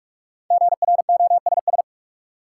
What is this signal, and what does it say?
Morse code, heard as a keyed tone, about 47 wpm, 700 Hz. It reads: GROSS
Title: Five Letter Words - 45wpm